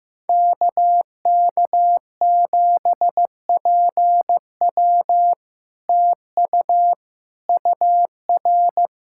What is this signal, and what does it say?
Morse code, heard as a keyed tone, about 15 wpm, 700 Hz. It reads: KK7PW TU UR
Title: SOTA QSO Practice 15wpm